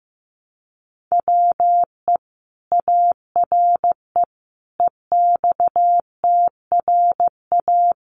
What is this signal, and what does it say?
Morse code, heard as a keyed tone, about 15 wpm, 700 Hz. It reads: WE ARE EXTRA
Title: Sentences from Top 1000 Words 15wpm